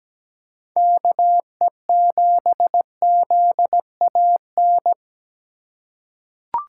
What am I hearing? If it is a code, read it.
Morse code, heard as a keyed tone, about 17 wpm, 700 Hz. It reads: KE7ZAN
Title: USA Call Signs - 6 Characters - 3x Character Spacing - 17wpm